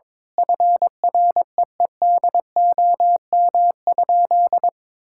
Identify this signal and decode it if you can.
Morse code, heard as a keyed tone, about 21 wpm, 700 Hz. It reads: FREEDOM?
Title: Sentences from Top 1600 Words 22wpm